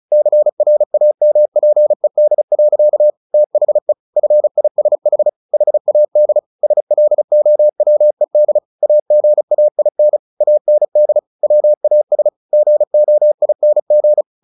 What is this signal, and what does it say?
Morse code, heard as a keyed tone, about 34 wpm, 600 Hz. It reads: CRAMPED. THE FISH HAD SLOWED AGAIN AND WAS GOING
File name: the old man and the sea / the old man and the sea 1 35wpm